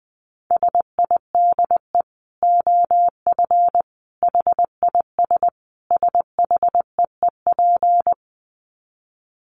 Morse code, heard as a keyed tone, about 19 wpm, 700 Hz. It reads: SIDE OF HIS SHEEP
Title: Dit Challenge 20wpm